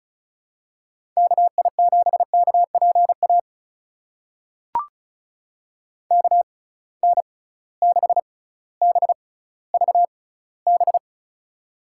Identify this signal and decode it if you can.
Morse code, heard as a keyed tone, about 34 wpm, spaced as for 18 wpm, 690 Hz. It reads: KI7KPA KN6BVB
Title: USA Call Signs - 6 Characters - 6x Character Spacing - 35wpm